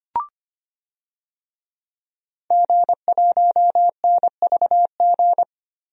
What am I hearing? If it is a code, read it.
Morse code, heard as a keyed tone, about 25 wpm, 700 Hz. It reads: G1NVG
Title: DXCC-100 Call Signs - 5 Characters - Repeated 2x - 25wpm